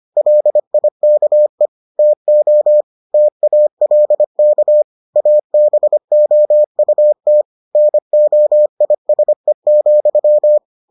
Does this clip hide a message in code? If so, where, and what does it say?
Yes: Morse code from 0.2 to 10.6 s, heard as a keyed tone, about 25 wpm, 600 Hz. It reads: LIKE TO TALK ABOUT NOISE,